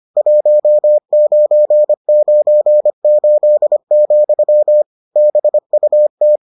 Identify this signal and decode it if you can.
Morse code, heard as a keyed tone, about 25 wpm, 600 Hz. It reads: 1998, BUT